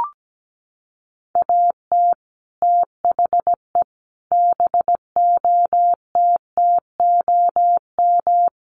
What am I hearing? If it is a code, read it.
Morse code, heard as a keyed tone, about 17 wpm, 700 Hz. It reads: AT THE BOTTOM
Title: Sets of 5 Words Encore2 17wpm